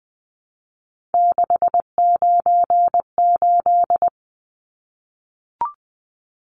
Morse code, heard as a keyed tone, about 20 wpm, 700 Hz. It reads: 698
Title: Numbers - 3-Digit 20wpm